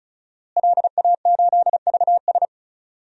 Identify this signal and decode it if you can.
Morse code, heard as a keyed tone, about 35 wpm, 700 Hz. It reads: LA8VS
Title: DXCC-100 Call Signs - 5 Characters - Repeated 3x - 35wpm